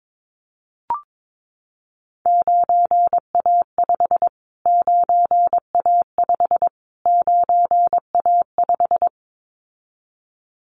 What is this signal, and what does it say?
Morse code, heard as a keyed tone, about 22 wpm, 700 Hz. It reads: 9A5 9A5 9A5
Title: Callsign Prefixes for Top 25 DXCC - Repeated 3x - 22wpm